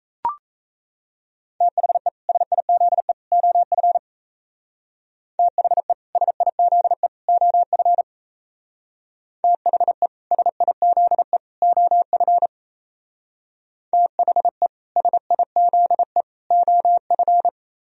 Morse code, heard as a keyed tone, about 37 wpm, 700 Hz. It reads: THE SIZE OF THE SIZE OF THE SIZE OF THE SIZE OF